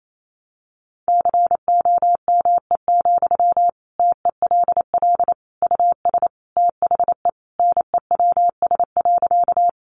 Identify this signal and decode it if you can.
Morse code, heard as a keyed tone, about 28 wpm, 700 Hz. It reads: COME, TELL US THE NEWS.
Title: Sentences from Top 800 Words 28wpm